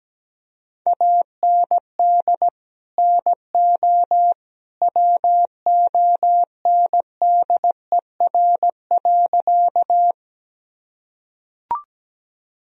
Morse code, heard as a keyed tone, about 17 wpm, 700 Hz. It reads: AND NO WONDER.
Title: Sentences from Top 1000 Words - Review Part 3 - 17wpm